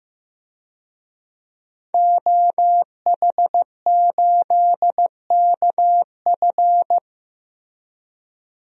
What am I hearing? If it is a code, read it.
Morse code, heard as a keyed tone, about 15 wpm, 700 Hz. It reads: OH8KF
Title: DXCC-100 Call Signs - 5 Characters - Repeated 3x - 15wpm